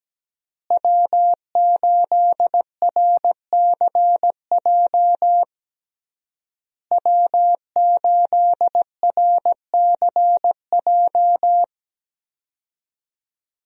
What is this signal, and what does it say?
Morse code, heard as a keyed tone, about 17 wpm, 700 Hz. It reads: W8RCJ W8RCJ
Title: USA Call Signs - 5 Characters - Repeated 3x - 17wpm